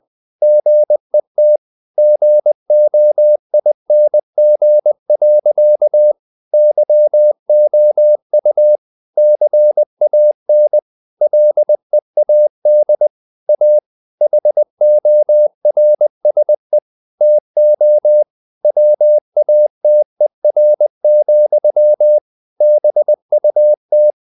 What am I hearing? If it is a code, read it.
Morse code, heard as a keyed tone, about 20 wpm, 600 Hz. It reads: GET GOING. YOU CAN LEAD A HORSE TO WATER, BUT
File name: CW / Practice / EnglishSayings20